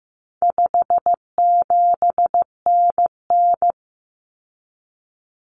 Morse code, heard as a keyed tone, about 15 wpm, 700 Hz. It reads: 57NN
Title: Common Cut Numbers - 4-Digit 15wpm